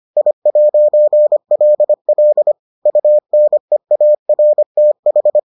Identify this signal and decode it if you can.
Morse code, heard as a keyed tone, about 25 wpm, 600 Hz. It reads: I'LL UNEARTH